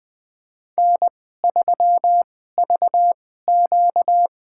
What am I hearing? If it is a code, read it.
Morse code, heard as a keyed tone, about 20 wpm, spaced as for 15 wpm, 700 Hz. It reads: N3VQ
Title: USA Call Signs - 4 Characters - 2x Character Spacing - 20wpm